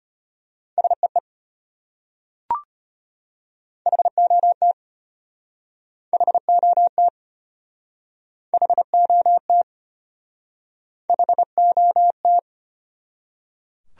Morse code, heard as a keyed tone, about 32 wpm, 700 Hz. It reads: SEE HOT HOT HOT HOT